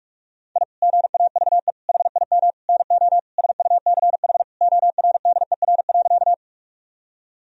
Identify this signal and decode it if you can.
Morse code, heard as a keyed tone, about 46 wpm, 700 Hz. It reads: I GAVE HIM NO SUCH ORDER.
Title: Sentences from Top 1000 Words - Review Part 2 - 45wpm